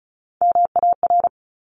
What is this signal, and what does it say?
Morse code, heard as a keyed tone, about 34 wpm, 700 Hz. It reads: MAR
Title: ARRL Sections 35wpm